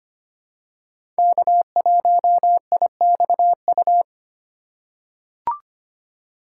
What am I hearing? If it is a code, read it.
Morse code, heard as a keyed tone, about 25 wpm, 700 Hz. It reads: K1IXU
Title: USA Call Signs - 5 Characters - 3x Character Spacing - 25wpm